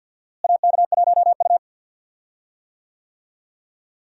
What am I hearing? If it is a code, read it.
Morse code, heard as a keyed tone, about 52 wpm, 700 Hz. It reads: AK1U
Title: USA Call Signs - 4 Characters - Repeated 2x - 50wpm